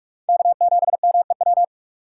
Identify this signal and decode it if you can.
Morse code, heard as a keyed tone, about 46 wpm, 700 Hz. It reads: K7MEW
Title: USA Call Signs - 5 Characters - 3x Character Spacing - 45wpm